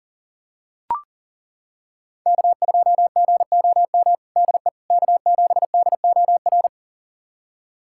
Morse code, heard as a keyed tone, about 37 wpm, 700 Hz. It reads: K2GOM DE K7DOR
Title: USA Callsign DE Callsign - 5 Characters - 40wpm